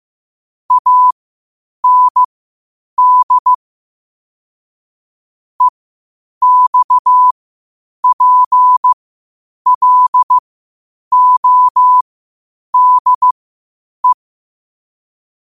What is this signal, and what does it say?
Morse code, heard as a keyed tone, about 15 wpm, spaced as for 8 wpm, 1000 Hz. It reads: AND EXPLODE
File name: qotdfiles / 8wpm